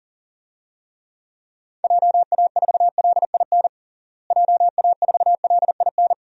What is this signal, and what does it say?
Morse code, heard as a keyed tone, about 40 wpm, 700 Hz. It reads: JA4LIN JA4LIN
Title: DXCC-100 Call Signs - 6 Characters - Repeated 2x - 40wpm